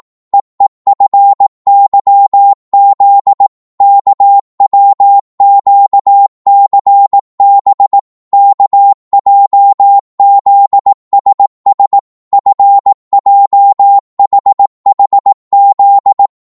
Morse code, heard as a keyed tone, about 18 wpm, 800 Hz. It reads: EEFYZ KWQCB KJZSS FJHHZ